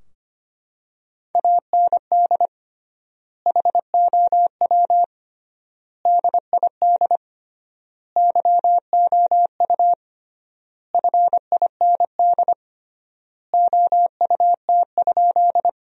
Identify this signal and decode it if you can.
Morse code, heard as a keyed tone, about 25 wpm, 700 Hz. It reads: AND HOW DID YOU FIND OUT?
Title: 3x Word-Spacing - Sentences from Top 100 Words - 25wpm